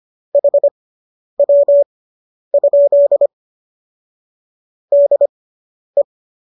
Morse code, heard as a keyed tone, about 25 wpm, spaced as for 10 wpm, 570 Hz. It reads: HW? DE